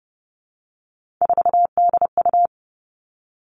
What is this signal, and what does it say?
Morse code, heard as a keyed tone, about 30 wpm, 700 Hz. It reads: 4DU
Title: Three Challenging Letter-Number Combinations 30wpm